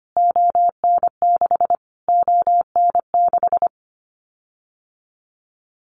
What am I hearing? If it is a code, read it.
Morse code, heard as a keyed tone, about 25 wpm, 700 Hz. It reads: ON6 ON6
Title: Callsign Prefixes for Top 25 DXCC - Repeated 3x - 25wpm